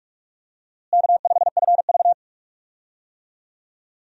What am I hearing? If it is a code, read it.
Morse code, heard as a keyed tone, about 44 wpm, 700 Hz. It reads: K5FV
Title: USA Call Signs - 4 Characters - Repeated 3x - 45wpm